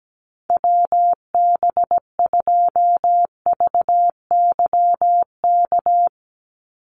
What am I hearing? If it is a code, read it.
Morse code, heard as a keyed tone, about 17 wpm, 700 Hz. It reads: WB2VYK
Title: Call Signs - 17wpm